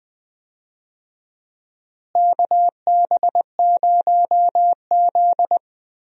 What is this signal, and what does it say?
Morse code, heard as a keyed tone, about 20 wpm, 700 Hz. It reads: KB0Z
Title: USA Call Signs - 4 Characters - Repeated 2x - 20wpm